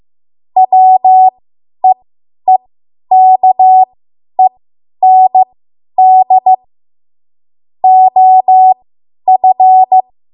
Morse code, heard as a keyed tone, about 15 wpm, spaced as for 10 wpm, 750 Hz. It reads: WEEKEND OF
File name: Morse / 10WPM